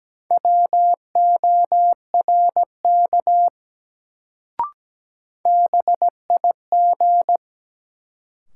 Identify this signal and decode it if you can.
Morse code, heard as a keyed tone, about 17 wpm, 700 Hz. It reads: WORK BIG
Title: Sentences from Top 200 Words - New Word Drill - 17wpm